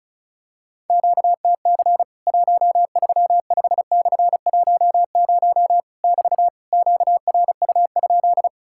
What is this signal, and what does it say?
Morse code, heard as a keyed tone, about 36 wpm, 700 Hz. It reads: QTC 135/10 <BT> QRU?
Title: Worked All Europe DX Contest - QTC Exchange - 35wpm